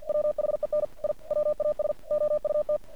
Morse code, heard as a keyed tone, about 50 wpm, 600 Hz. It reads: WHEN I WAS OUT
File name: archived / CRSnet